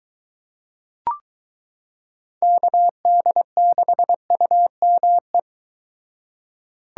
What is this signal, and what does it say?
Morse code, heard as a keyed tone, about 23 wpm, 700 Hz. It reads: KD6UME